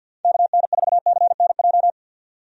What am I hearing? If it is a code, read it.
Morse code, heard as a keyed tone, about 51 wpm, 700 Hz. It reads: KN4CNJ
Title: USA Call Signs - 6 Characters - 50wpm